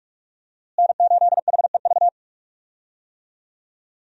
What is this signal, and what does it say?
Morse code, heard as a keyed tone, about 46 wpm, 700 Hz. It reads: N8HEV
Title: USA Call Signs - 5 Characters - Repeated 3x - 45wpm